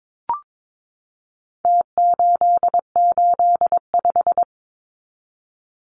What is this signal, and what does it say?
Morse code, heard as a keyed tone, about 22 wpm, 700 Hz. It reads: T885
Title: Numbers - Contest Style - 4-Digit 22wpm